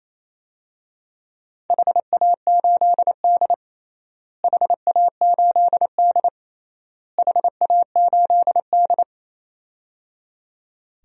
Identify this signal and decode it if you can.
Morse code, heard as a keyed tone, about 28 wpm, 700 Hz. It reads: HA8D HA8D HA8D
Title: DXCC-100 Call Signs - 4 Characters - Repeated 3x - 28wpm